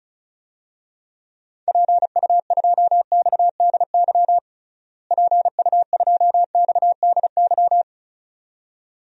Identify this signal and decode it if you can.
Morse code, heard as a keyed tone, about 36 wpm, 700 Hz. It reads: PU2XDY PU2XDY
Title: DXCC-100 Call Signs - 6 Characters - Repeated 2x - 35wpm